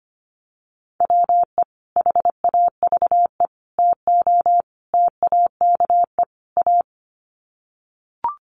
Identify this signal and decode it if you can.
Morse code, heard as a keyed tone, about 25 wpm, 700 Hz. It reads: WE HAVE TO TAKE A
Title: Sets of 5 Words Encore 25wpm